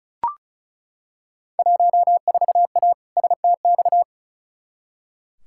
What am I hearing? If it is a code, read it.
Morse code, heard as a keyed tone, about 34 wpm, 700 Hz. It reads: 14A STX
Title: Field Day Exchanges Encore - 35wpm